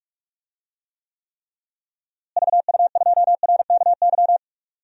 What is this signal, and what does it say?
Morse code, heard as a keyed tone, about 43 wpm, 700 Hz. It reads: VU2RKY
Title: DXCC-100 Call Signs - 6 Characters - Repeated 3x - 45wpm